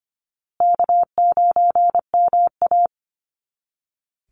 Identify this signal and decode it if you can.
Morse code, heard as a keyed tone, about 25 wpm, 700 Hz. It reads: K9MA
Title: Call Signs Repeated 2x - 25wpm